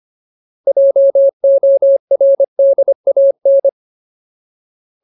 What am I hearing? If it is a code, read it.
Morse code, heard as a keyed tone, about 25 wpm, 550 Hz. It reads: JORDAN
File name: Ultimate Word List / 25WPM Ultimate